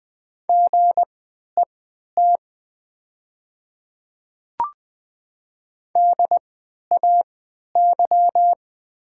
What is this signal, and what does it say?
Morse code, heard as a keyed tone, about 20 wpm, spaced as for 11 wpm, 700 Hz. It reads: GET DAY
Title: 3x Character Spacing - Top 200 Words - 20wpm